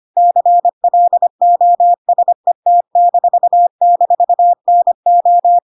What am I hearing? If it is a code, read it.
Morse code, heard as a keyed tone, about 25 wpm, 700 Hz. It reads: CLOSET--NO
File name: Pinocchio / Pinocchio ch1 25 25 ik1hge